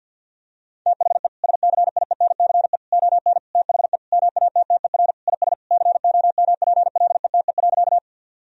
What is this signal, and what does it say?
Morse code, heard as a keyed tone, about 50 wpm, 700 Hz. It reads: THE SCIENCE ON THE MATTER IS COMPLETE.
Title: Sentences from Top 1000 Words - Review Part 2 - 50wpm